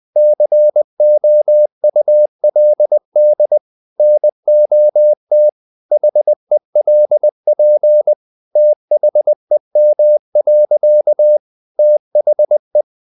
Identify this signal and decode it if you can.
Morse code, heard as a keyed tone, about 20 wpm, 600 Hz. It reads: COULD NOT HELP THEM. THE
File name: the old man and the sea / the old man and the sea part 1 20wpm